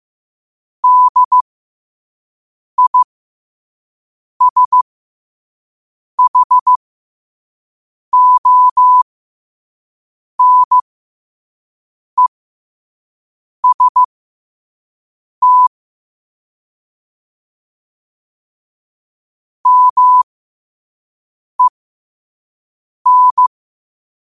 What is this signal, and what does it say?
Morse code, heard as a keyed tone, about 15 wpm, spaced as for 5 wpm, 1000 Hz. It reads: DISHONEST MEN
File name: qotdfiles / 5wpm